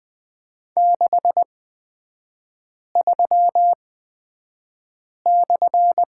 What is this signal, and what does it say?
Morse code, heard as a keyed tone, about 20 wpm, spaced as for 5 wpm, 700 Hz. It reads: 63/
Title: 224 Lesson 36e - BT - Three Characters 20wpm